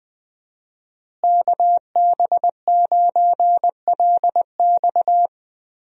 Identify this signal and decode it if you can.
Morse code, heard as a keyed tone, about 20 wpm, 700 Hz. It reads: KB9LX
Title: USA Call Signs - 5 Characters - Repeated 2x - 20wpm